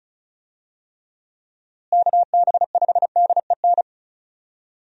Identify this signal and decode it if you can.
Morse code, heard as a keyed tone, about 36 wpm, 700 Hz. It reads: KB5DEN
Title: USA Call Signs - 6 Characters - Repeated 2x - 35wpm